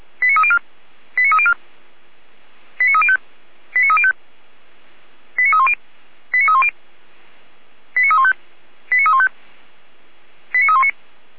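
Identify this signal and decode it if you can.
Five-tone selective call (ZVEI), heard as quick runs of five tones, tones of about 70 ms. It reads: 89304 89304 89305 89305 89310 89310 89315 89315 89319
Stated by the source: ZVEI five-tone call received on an SDR.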